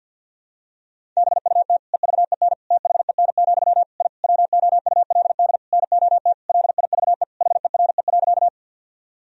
Morse code, heard as a keyed tone, about 53 wpm, 700 Hz. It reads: BUT EVEN THEN, I WOULD NOT LIVE HERE.
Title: Sentences from Top 1000 Words - Review Part 1 - 50wpm